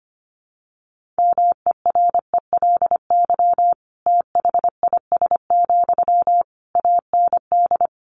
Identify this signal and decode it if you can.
Morse code, heard as a keyed tone, about 25 wpm, 700 Hz. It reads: MERELY THIS, AND
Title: Sentences from Top 1400 Words 25wpm